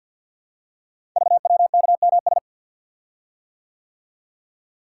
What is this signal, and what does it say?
Morse code, heard as a keyed tone, about 47 wpm, 700 Hz. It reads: VWKMS
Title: Code Groups - 5 Letters - 50wpm